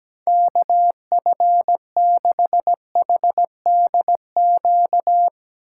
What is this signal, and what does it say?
Morse code, heard as a keyed tone, about 19 wpm, 700 Hz. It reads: KF6HDQ
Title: Speed Racing - USA 2x3 Call Signs 17wpm